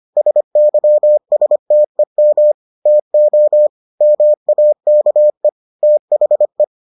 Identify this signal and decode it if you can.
Morse code, heard as a keyed tone, about 25 wpm, 600 Hz. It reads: SYSTEM TO MAKE THE